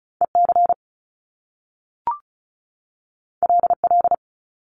Morse code, heard as a keyed tone, about 36 wpm, 700 Hz. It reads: EC LL
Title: Top Two Letter Combinations in Words 35wpm